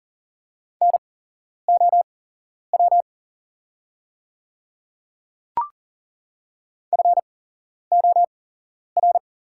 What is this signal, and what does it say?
Morse code, heard as a keyed tone, about 40 wpm, spaced as for 11 wpm, 700 Hz. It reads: NOW FOR
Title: 8x Character Spacing - Top 100 Words - 40wpm